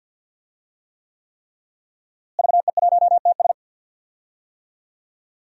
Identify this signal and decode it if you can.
Morse code, heard as a keyed tone, about 51 wpm, 700 Hz. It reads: VE1TS
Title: DXCC-100 Call Signs - 5 Characters - Repeated 3x - 50wpm